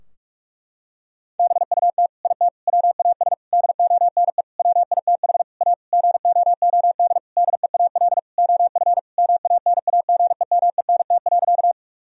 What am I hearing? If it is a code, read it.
Morse code, heard as a keyed tone, about 44 wpm, 700 Hz. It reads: BUT IT WAS DONE WITH A GOOD DEAL OF MANAGEMENT.
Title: Sentences from Top 1000 Words - Review Part 4 - 45wpm